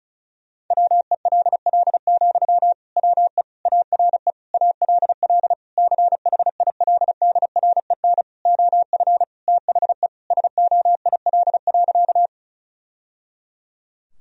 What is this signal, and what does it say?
Morse code, heard as a keyed tone, about 36 wpm, 700 Hz. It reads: WELL, WE ARE ALL CHILDREN OF THE SOIL.
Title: Sentences from Top 1000 Words - Review Part 4 - 35wpm